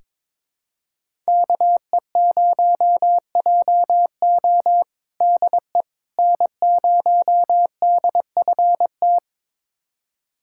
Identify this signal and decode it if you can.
Morse code, heard as a keyed tone, about 21 wpm, 700 Hz. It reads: KE0JO DE N0DFT
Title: USA Callsign DE Callsign - 5 Characters - 22wpm